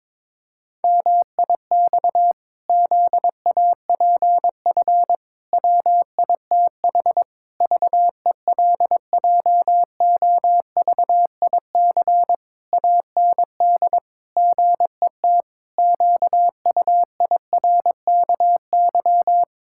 Morse code, heard as a keyed tone, about 22 wpm, 700 Hz. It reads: MIX ZAPF WITH VELJOVIC AND GET QUIRKY
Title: Pangram Challenge v2 - 22wpm